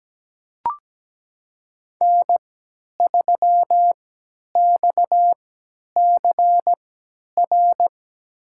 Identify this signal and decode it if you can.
Morse code, heard as a keyed tone, about 17 wpm, spaced as for 10 wpm, 700 Hz. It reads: N3XCR
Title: USA Call Signs - 3x Character Spacing - 17wpm